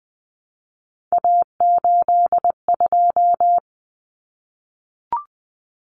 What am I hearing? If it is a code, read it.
Morse code, heard as a keyed tone, about 20 wpm, 700 Hz. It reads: A82
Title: Common Cut Numbers - 3-Digit 20wpm